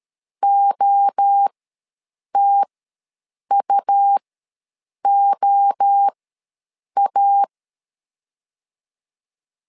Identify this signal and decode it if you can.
Morse code, heard as a keyed tone, about 13 wpm, spaced as for 7 wpm, 800 Hz. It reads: OTUOA
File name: LEZIONI / 3 MWRO / 3 MWRO ws 12wpm 1914